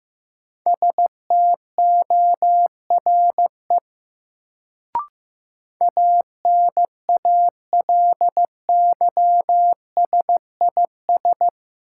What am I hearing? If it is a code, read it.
Morse code, heard as a keyed tone, about 15 wpm, 700 Hz. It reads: STORE ANALYSIS
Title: Sentences from Top 1000 Words - Review Part 5 - New Word Drill - 15wpm